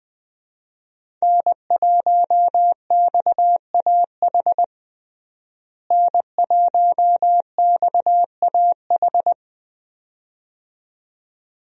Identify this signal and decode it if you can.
Morse code, heard as a keyed tone, about 20 wpm, 700 Hz. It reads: N1XAH N1XAH
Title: USA Call Signs - 5 Characters - Repeated 2x - 20wpm